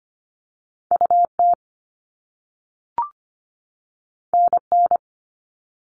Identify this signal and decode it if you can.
Morse code, heard as a keyed tone, about 25 wpm, 700 Hz. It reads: U T N N
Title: All Cut Numbers - 2-Digit 25wpm